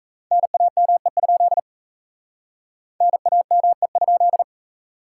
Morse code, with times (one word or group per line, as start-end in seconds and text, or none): 0.3-1.6 NAME?
3.0-4.4 NAME?